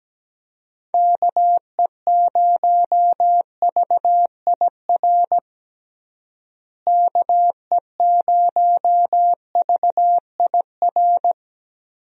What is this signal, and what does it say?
Morse code, heard as a keyed tone, about 17 wpm, 700 Hz. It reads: KE0VIR KE0VIR
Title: USA Call Signs - 6 Characters - Repeated 3x - 17wpm